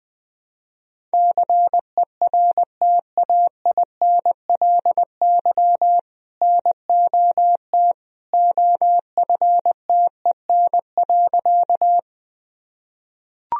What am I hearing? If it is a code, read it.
Morse code, heard as a keyed tone, about 20 wpm, 700 Hz. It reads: CERTAINLY NOT OFTEN.
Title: Sentences from Top 1000 Words - Review Part 1 - 20wpm